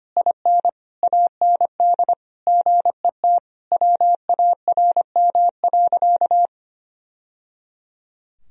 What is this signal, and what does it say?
Morse code, heard as a keyed tone, about 25 wpm, 700 Hz. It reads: IN AND GET WARM.
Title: Sentences from Top 1000 Words - Review Part 2 - 25wpm